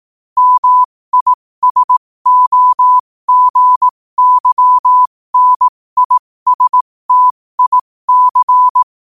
Morse code, heard as a keyed tone, about 18 wpm, spaced as for 15 wpm, 1000 Hz. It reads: MISOGYNISTIC
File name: qotdfiles / 15wpm